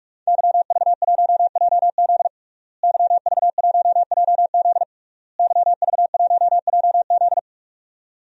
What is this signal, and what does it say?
Morse code, heard as a keyed tone, about 46 wpm, 700 Hz. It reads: YV1JZ YV1JZ YV1JZ
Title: DXCC-100 Call Signs - 5 Characters - Repeated 3x - 45wpm